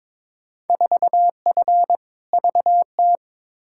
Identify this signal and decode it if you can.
Morse code, heard as a keyed tone, about 22 wpm, 700 Hz. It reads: 4F VT
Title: Field Day Exchanges Encore - 22wpm